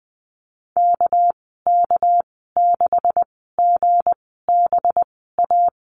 Morse code, heard as a keyed tone, about 20 wpm, spaced as for 14 wpm, 700 Hz. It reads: KK6GBA
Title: 219 Lesson 35g - Number 0 - Call Signs 20wpm